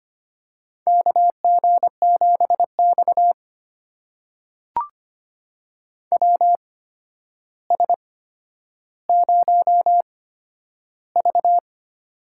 Morse code, heard as a keyed tone, about 25 wpm, spaced as for 11 wpm, 700 Hz. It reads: KG7X WS0V
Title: USA Call Signs - 4 Characters - 8x Character Spacing - 25wpm